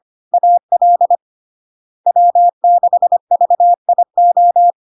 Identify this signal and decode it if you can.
Morse code, heard as a keyed tone, about 25 wpm, 700 Hz. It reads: AL W6VIO
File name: LARK-FD-2022-Log-in-CW-25wpm-700-Hz